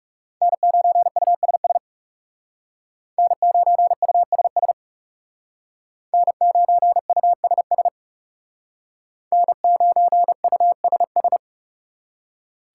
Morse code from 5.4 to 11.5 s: N9USS N9USS

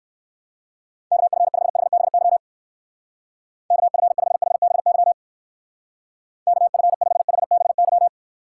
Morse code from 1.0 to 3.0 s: KF5HBY

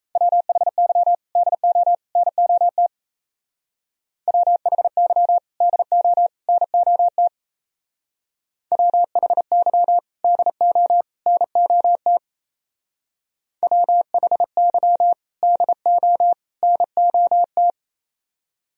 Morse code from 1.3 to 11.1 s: DO NOT WHY DO NOT WHY DO

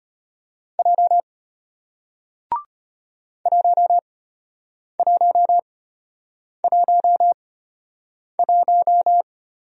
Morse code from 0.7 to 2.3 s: J

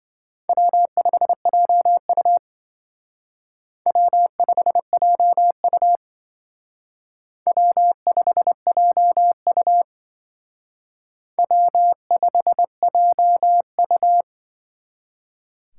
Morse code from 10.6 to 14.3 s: W5JU

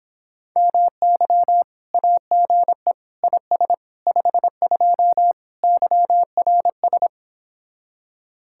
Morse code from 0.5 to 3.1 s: MY AGE

